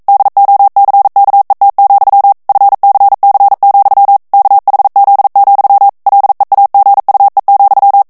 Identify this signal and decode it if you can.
Morse code, heard as a keyed tone, about 42 wpm, 800 Hz. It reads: DOCKET, FCC, KHZ, LEAGUE,
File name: 40 mots minute